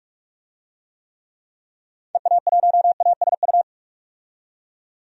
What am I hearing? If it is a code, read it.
Morse code, heard as a keyed tone, about 46 wpm, 700 Hz. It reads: EA1ASU
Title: DXCC-100 Call Signs - 6 Characters - 45wpm